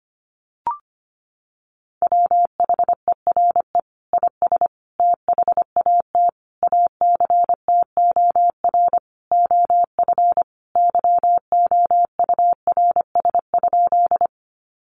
Morse code, heard as a keyed tone, about 25 wpm, 700 Hz. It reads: WHERE IS THAT ACTOR OF YOURS?
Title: Sentences from Top 1500 Words 25wpm